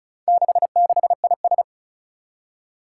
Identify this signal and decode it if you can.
Morse code, heard as a keyed tone, about 36 wpm, 690 Hz. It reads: 66IS
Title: Four Challenging Letter-Number Combinations 35wpm